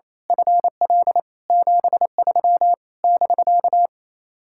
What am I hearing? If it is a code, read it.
Morse code, heard as a keyed tone, about 27 wpm, 700 Hz. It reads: FL 73 <BK>
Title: Speed-Racing Common QSO Phrases Encore - 28wpm